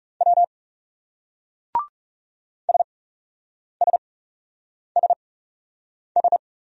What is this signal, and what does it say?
Morse code, heard as a keyed tone, about 40 wpm, spaced as for 8 wpm, 700 Hz. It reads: W SSSS